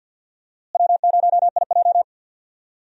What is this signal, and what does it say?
Morse code, heard as a keyed tone, about 53 wpm, 690 Hz. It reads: W0IJ
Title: USA Call Signs - 4 Characters - Repeated 2x - 50wpm